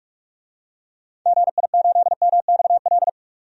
Morse code, heard as a keyed tone, about 46 wpm, 700 Hz. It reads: GI8MXL
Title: DXCC-100 Call Signs - 6 Characters - 45wpm